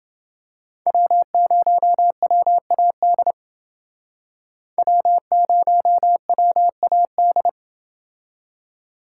Morse code, heard as a keyed tone, about 29 wpm, 700 Hz. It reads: W0WAD W0WAD